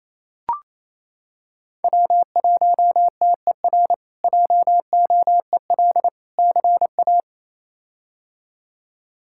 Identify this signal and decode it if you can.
Morse code, heard as a keyed tone, about 27 wpm, 700 Hz. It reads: W1TER JOEL CA
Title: K1USN SST QSO Practice - 28wpm